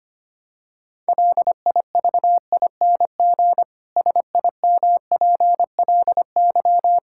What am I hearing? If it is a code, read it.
Morse code, heard as a keyed tone, about 25 wpm, 700 Hz. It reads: LIVING SIMPLY
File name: practice / 25wpm / Difficult Sentences with Hints - 25wpm